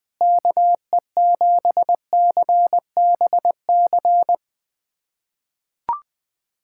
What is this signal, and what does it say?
Morse code, heard as a keyed tone, about 21 wpm, 700 Hz. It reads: KE7CBC
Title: USA Callsign DE Callsign 20wpm